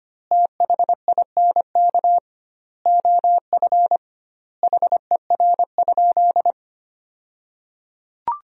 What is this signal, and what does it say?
Morse code, heard as a keyed tone, about 25 wpm, 700 Hz. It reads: THINK OF HER?
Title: 2x Word-Spacing - Sentences from Top 200 Words - 25wpm